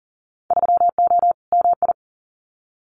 Morse code, heard as a keyed tone, about 40 wpm, 700 Hz. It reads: 3O MI
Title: Winter Field Day Exchanges 40wpm